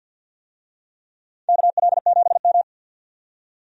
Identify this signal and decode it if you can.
Morse code, heard as a keyed tone, about 52 wpm, 700 Hz. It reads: KL7M
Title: DXCC-100 Call Signs - 4 Characters - 50wpm